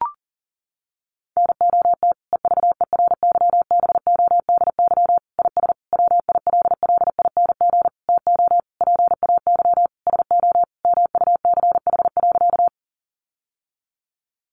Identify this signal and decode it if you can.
Morse code, heard as a keyed tone, about 40 wpm, 700 Hz. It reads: NOT EVERYBODY IS WILLING TO PAY SO MUCH.
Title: Sentences from Top 1400 Words 40wpm